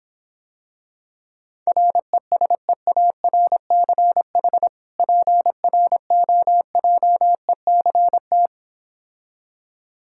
Morse code, heard as a keyed tone, about 26 wpm, 700 Hz. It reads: RESEARCH PROJECT